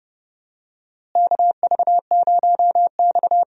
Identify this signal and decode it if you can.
Morse code, heard as a keyed tone, about 31 wpm, 700 Hz. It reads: KV0X
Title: USA Call Signs - 4 Characters - Repeated 2x - 30wpm